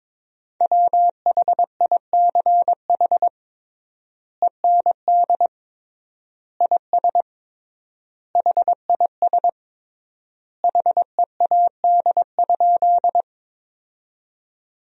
Morse code, heard as a keyed tone, about 22 wpm, 700 Hz. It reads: WHICH END IS HIS HEAD?
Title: 3x Word-Spacing - Sentences from Top 300 Words - 22wpm